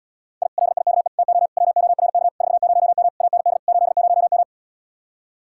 Morse code, heard as a keyed tone, about 79 wpm, 700 Hz. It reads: I LIKE TO WORK 40M AND 80M